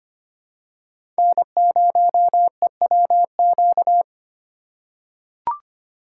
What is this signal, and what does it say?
Morse code, heard as a keyed tone, about 25 wpm, 700 Hz. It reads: N0EWQ
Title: USA Call Signs - 6x Character Spacing - 25wpm